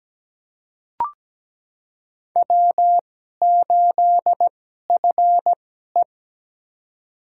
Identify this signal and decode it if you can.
Morse code, heard as a keyed tone, about 17 wpm, spaced as for 12 wpm, 700 Hz. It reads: W8FE
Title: USA Call Signs - 4 Characters - 2x Character Spacing - 17wpm